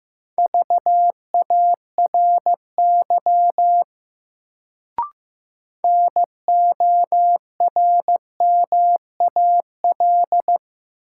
Morse code, heard as a keyed tone, about 15 wpm, 700 Hz. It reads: VARY NORMAL
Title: Sentences from Top 1200 Words - New Word Drill Alt - 15wpm